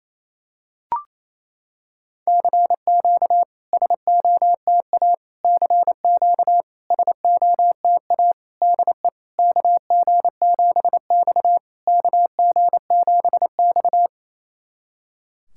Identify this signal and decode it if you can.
Morse code, heard as a keyed tone, about 28 wpm, 700 Hz. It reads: CQ SOTA CQ SOTA DE KG7X KG7X
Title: SOTA QSO Practice 28wpm